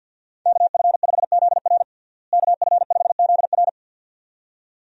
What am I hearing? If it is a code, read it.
Morse code, heard as a keyed tone, about 51 wpm, 700 Hz. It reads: KF5ZR KF5ZR
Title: USA Call Signs - 5 Characters - Repeated 2x - 50wpm